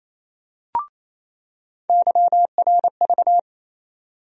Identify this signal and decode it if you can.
Morse code, heard as a keyed tone, about 28 wpm, 700 Hz. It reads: YRV
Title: USA Call Sign Postfixes - 3 Letters 28wpm